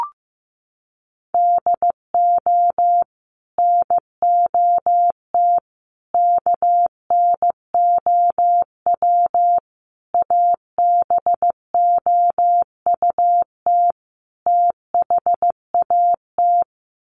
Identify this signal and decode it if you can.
Morse code, heard as a keyed tone, about 15 wpm, 700 Hz. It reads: DO NOT KNOW ABOUT THAT
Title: Sets of 5 Words Encore2 15wpm